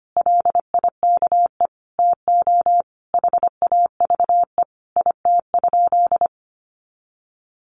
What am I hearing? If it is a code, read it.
Morse code, heard as a keyed tone, about 25 wpm, 700 Hz. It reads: LIKE TO HAVE IT?
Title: Sentences from Top 100 Words 25wpm